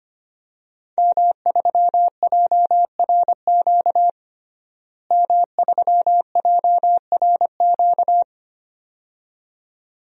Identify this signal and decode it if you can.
Morse code, heard as a keyed tone, about 25 wpm, 700 Hz. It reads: M3JRQ M3JRQ
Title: DXCC-100 Call Signs - 5 Characters - Repeated 2x - 25wpm